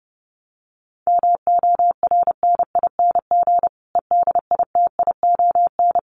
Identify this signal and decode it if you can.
Morse code, heard as a keyed tone, about 30 wpm, 700 Hz. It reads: MORNING EDITION
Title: Sets of 5 Words Encore 30wpm